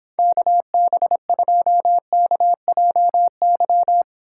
Morse code, heard as a keyed tone, about 26 wpm, 700 Hz. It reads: KB2KJY